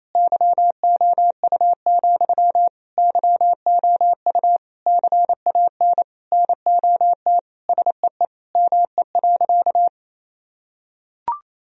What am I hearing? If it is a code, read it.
Morse code, heard as a keyed tone, about 28 wpm, 700 Hz. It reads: YOU, YOU CAN NOT SEE ME.
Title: Sentences from Top 1000 Words - Review Part 3 - 28wpm